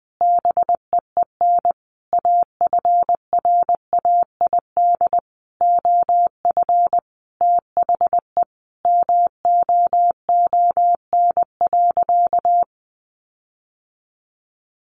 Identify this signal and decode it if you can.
Morse code, heard as a keyed tone, about 20 wpm, 700 Hz. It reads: BEEN AFRAID OF THE MOON.
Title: Sentences from Top 1500 Words 20wpm